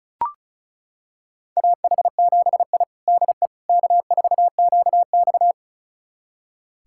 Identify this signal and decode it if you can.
Morse code, heard as a keyed tone, about 36 wpm, 700 Hz. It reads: AH7I DE K4QX
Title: USA Callsign DE Callsign - 4 Characters - 35wpm